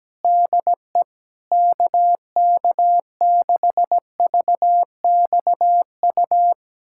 Morse code, heard as a keyed tone, about 17 wpm, 700 Hz. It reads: DE KK6VXU
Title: USA Callsign DE Callsign - 6 Characters - 17wpm